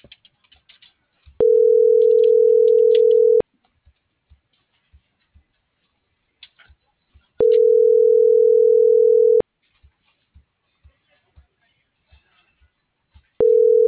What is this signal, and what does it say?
Ringback tone (North America), 440 + 480 Hz.